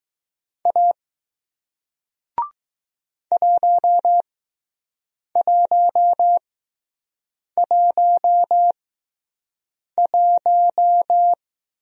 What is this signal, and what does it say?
Morse code, heard as a keyed tone, about 20 wpm, spaced as for 6 wpm, 700 Hz. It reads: A 1111